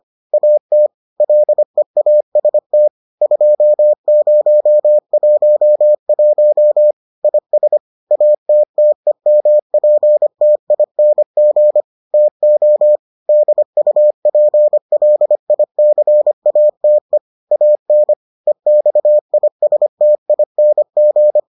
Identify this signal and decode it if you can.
Morse code, heard as a keyed tone, about 25 wpm, 600 Hz. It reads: AT LEAST 2011 IS ATTEMPTING TO DUPLICATE AN EXISTING